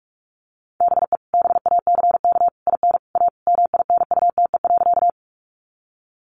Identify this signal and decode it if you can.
Morse code, heard as a keyed tone, about 44 wpm, 700 Hz. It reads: BE BACK IN A MINUTE.
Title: Sentences from Top 500 Words 45wpm